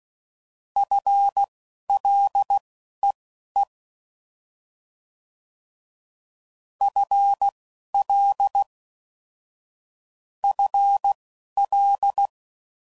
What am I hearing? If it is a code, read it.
Morse code, heard as a keyed tone, about 16 wpm, spaced as for 11 wpm, 800 Hz. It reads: FLEE FL FL
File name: Prefixes / FLs & THs / FL Prefixes 3